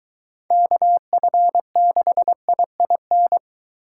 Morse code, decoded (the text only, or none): KF6IIN